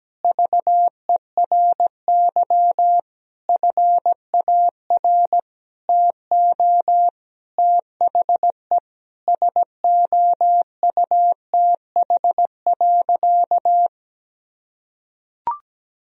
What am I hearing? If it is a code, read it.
Morse code, heard as a keyed tone, about 17 wpm, 700 Hz. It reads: VERY FAR TO THE SOUTH.
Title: Sentences from Top 1000 Words - Review Part 2 - 17wpm